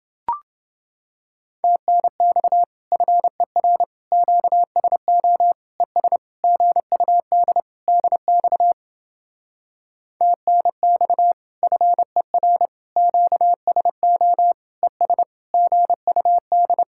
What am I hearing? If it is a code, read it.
Morse code, heard as a keyed tone, about 29 wpm, 700 Hz. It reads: TNX FER QSO ES GUD DX TNX FER QSO ES GUD